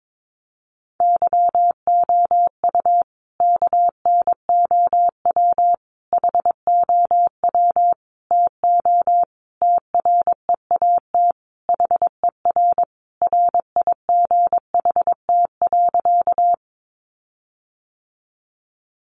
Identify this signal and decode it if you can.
Morse code, heard as a keyed tone, about 22 wpm, 700 Hz. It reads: YOU KNOW HOW TO TREAT HER RIGHT.
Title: Sentences from Top 1300 Words 22wpm